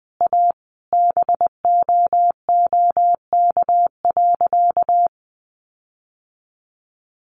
Morse code, heard as a keyed tone, about 20 wpm, 700 Hz. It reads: A BOOK.
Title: Sentences from Top 1500 Words 20wpm